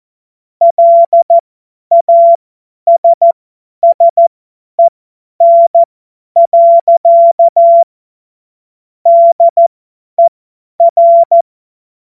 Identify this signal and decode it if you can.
Morse code, heard as a keyed tone, about 14 wpm, spaced as for 10 wpm, 680 Hz. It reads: LASSEN. DER
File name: Morsekurs / Witze / Witz 01. November 2022 50 F70